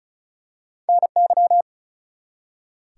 Morse code, heard as a keyed tone, about 36 wpm, 700 Hz. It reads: NY
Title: US States and Canadian Provinces 35wpm